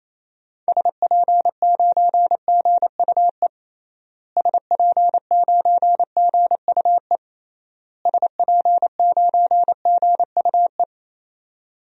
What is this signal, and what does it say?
Morse code, heard as a keyed tone, about 28 wpm, 700 Hz. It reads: SP9GUE SP9GUE SP9GUE
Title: DXCC-100 Call Signs - 6 Characters - Repeated 3x - 28wpm